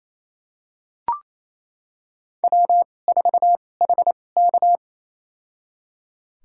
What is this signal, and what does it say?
Morse code, heard as a keyed tone, about 28 wpm, spaced as for 20 wpm, 700 Hz. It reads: W4HK
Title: USA Call Signs - 4 Characters - 2x Character Spacing - 28wpm